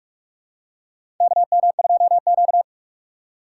Morse code, heard as a keyed tone, about 46 wpm, 700 Hz. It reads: KM2Q
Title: USA Call Signs - 6x Character Spacing - 45wpm